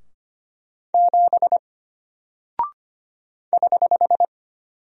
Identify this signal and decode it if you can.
Morse code, heard as a keyed tone, about 25 wpm, 700 Hz. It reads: 7 <HH>
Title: Single Character (Letter:Number:Punctuation:Prosign) Alt 25wpm